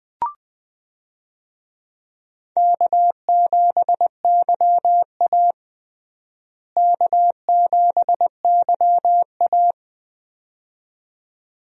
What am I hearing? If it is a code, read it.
Morse code, heard as a keyed tone, about 20 wpm, 700 Hz. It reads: K7YA K7YA
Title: USA Call Signs - 4 Characters - Repeated 2x - 20wpm